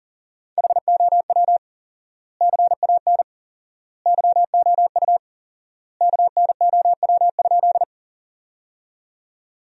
Morse code, heard as a keyed tone, about 40 wpm, 700 Hz. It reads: HOW CAN YOU KNOW?
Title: 4x Word-Spacing - Sentences from Top 100 Words - 40wpm